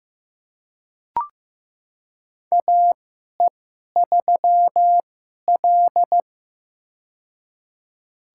Morse code, heard as a keyed tone, about 15 wpm, spaced as for 11 wpm, 700 Hz. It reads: AE3L
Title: USA Call Signs - 4 Characters - 2x Character Spacing - 15wpm